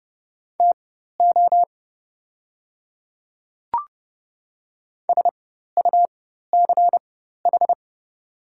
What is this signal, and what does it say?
Morse code, heard as a keyed tone, about 30 wpm, spaced as for 14 wpm, 700 Hz. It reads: TO SUCH
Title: 4x Character Spacing - Top 200 Words - 30wpm